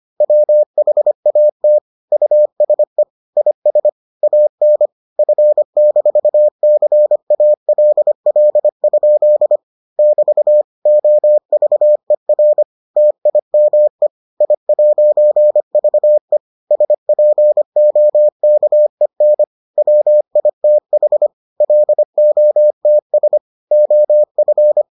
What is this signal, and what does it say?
Morse code, heard as a keyed tone, about 25 wpm, 600 Hz. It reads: WHAT USE IS AN F-CALL? <BT> OVER TIME I'VE SPOKEN WITH LOTS OF